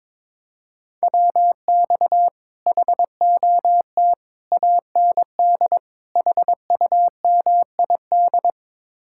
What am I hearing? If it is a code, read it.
Morse code, heard as a keyed tone, about 23 wpm, 700 Hz. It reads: WX HOT AND HUMID
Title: Speed-Racing Common QSO Phrases Encore - 22wpm